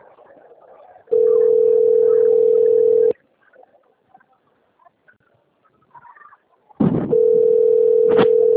Ringback tone (North America), 440 + 480 Hz.